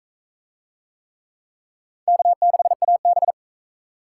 Morse code, heard as a keyed tone, about 42 wpm, 700 Hz. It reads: K6AB